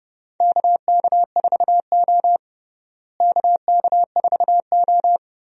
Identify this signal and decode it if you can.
Morse code, heard as a keyed tone, about 30 wpm, 700 Hz. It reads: KK4O KK4O
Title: USA Call Signs - 4 Characters - Repeated 2x - 30wpm